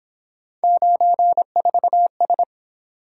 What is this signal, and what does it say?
Morse code, heard as a keyed tone, about 26 wpm, 700 Hz. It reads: 94S
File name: Speed-Racing - Three Letter-Number Combinations - 22wpm